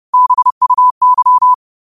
Morse code, heard as a keyed tone, about 30 wpm, 1000 Hz. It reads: BUY
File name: qotdfiles / 30wpm